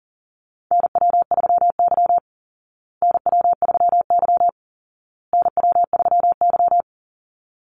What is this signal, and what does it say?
Morse code, heard as a keyed tone, about 40 wpm, 700 Hz. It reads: NW3Y NW3Y NW3Y
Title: Call Signs Repeated 2x - 40wpm